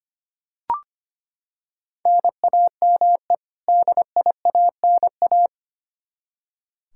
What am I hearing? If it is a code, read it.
Morse code, heard as a keyed tone, about 25 wpm, 700 Hz. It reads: NAME DIANA
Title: Practice QSO - Exchange 1 - Beginning - 25wpm